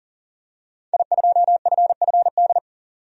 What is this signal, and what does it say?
Morse code, heard as a keyed tone, about 40 wpm, 700 Hz. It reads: I2FFD
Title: DXCC-100 Call Signs - 5 Characters - 40wpm